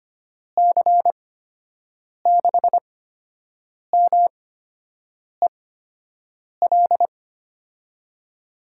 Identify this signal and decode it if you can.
Morse code, heard as a keyed tone, about 25 wpm, spaced as for 7 wpm, 700 Hz. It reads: C6MEL
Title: USA Call Signs - 6 Characters - 8x Character Spacing - 25wpm